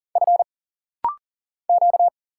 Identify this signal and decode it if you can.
Morse code, heard as a keyed tone, about 40 wpm, 700 Hz. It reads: F Q
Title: Single Letters - Rapid-Fire 40wpm